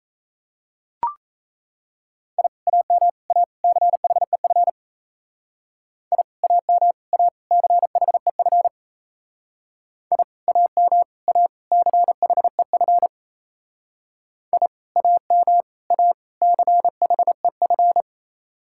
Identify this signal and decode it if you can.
Morse code, heard as a keyed tone, about 34 wpm, 700 Hz. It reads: I AM A CHEF I AM A CHEF I AM A CHEF I AM A CHEF
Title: Speed-Racing Common QSO Phrases Encore - 28wpm